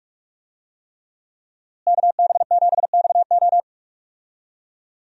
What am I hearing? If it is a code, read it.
Morse code, heard as a keyed tone, about 46 wpm, 700 Hz. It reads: KB7XO
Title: USA Call Signs - 5 Characters - Repeated 3x - 45wpm